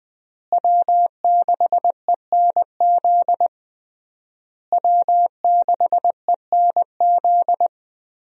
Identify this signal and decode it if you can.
Morse code, heard as a keyed tone, about 20 wpm, 700 Hz. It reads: W6ENZ W6ENZ
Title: USA Call Signs - 5 Characters - Repeated 3x - 20wpm